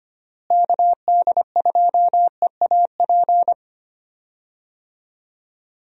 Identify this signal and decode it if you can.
Morse code, heard as a keyed tone, about 25 wpm, 700 Hz. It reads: KD2EAP
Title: USA Call Signs - 6 Characters - Repeated 2x - 25wpm